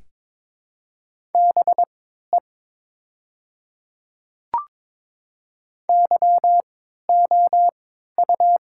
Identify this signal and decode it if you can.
Morse code, heard as a keyed tone, about 22 wpm, spaced as for 12 wpm, 700 Hz. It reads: BE YOU
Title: 3x Character Spacing - Top 200 Words - 22wpm